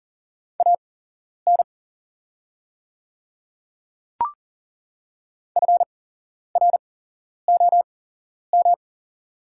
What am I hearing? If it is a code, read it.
Morse code, heard as a keyed tone, about 40 wpm, spaced as for 11 wpm, 690 Hz. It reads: AN FROM
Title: 8x Character Spacing - Top 100 Words - 40wpm